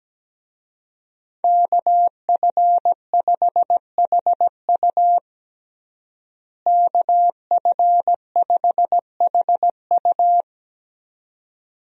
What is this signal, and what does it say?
Morse code, heard as a keyed tone, about 17 wpm, 700 Hz. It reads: KF5HU KF5HU
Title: USA Call Signs - 5 Characters - Repeated 2x - 17wpm